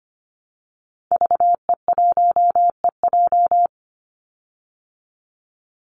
Morse code, heard as a keyed tone, about 25 wpm, 700 Hz. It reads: VE1EJ
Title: Call Signs Repeated 3x - 25wpm